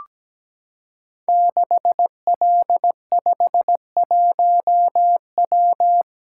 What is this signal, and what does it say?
Morse code, heard as a keyed tone, about 17 wpm, 700 Hz. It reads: 6L51W
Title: Code Groups - 5 Letter:Numbers - 17wpm